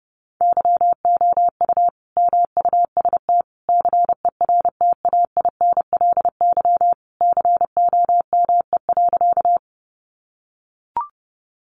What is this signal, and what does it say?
Morse code, heard as a keyed tone, about 30 wpm, 700 Hz. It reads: YOU MUST CERTAINLY COME.
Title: Sentences from Top 900 Words 30wpm